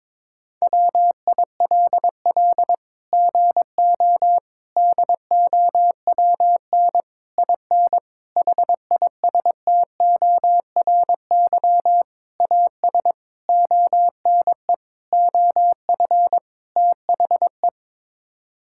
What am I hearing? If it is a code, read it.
Morse code, heard as a keyed tone, about 22 wpm, 700 Hz. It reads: WILL GO DOWN IN HISTORY AS ONE OF THE
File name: Sets of 9 Words - 22wpm